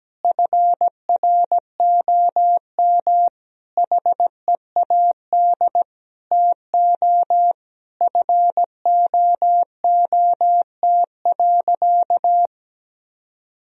Morse code, heard as a keyed tone, about 17 wpm, 700 Hz. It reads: FROM HEAD TO FOOT.
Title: Sentences from Top 1000 Words - Review Part 3 - 17wpm